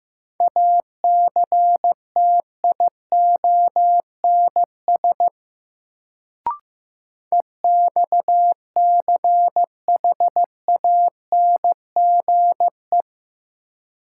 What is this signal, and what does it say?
Morse code, heard as a keyed tone, about 15 wpm, 700 Hz. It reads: ACTIONS EXCHANGE
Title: Sentences from Top 1000 Words - Review Part 6 - New Word Drill - 15wpm